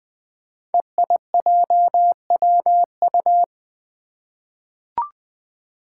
Morse code, heard as a keyed tone, about 20 wpm, 700 Hz. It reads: EIJWU
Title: Five Challenging Letter-Number Combinations 20wpm